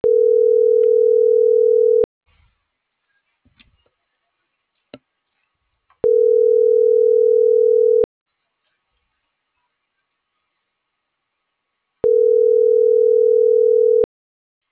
Ringback tone (North America), 440 + 480 Hz.